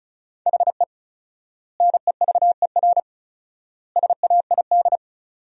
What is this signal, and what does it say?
Morse code, heard as a keyed tone, about 34 wpm, 700 Hz. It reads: HE NEVER SAID
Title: 4x Word-Spacing - Sentences from Top 500 Words - 35wpm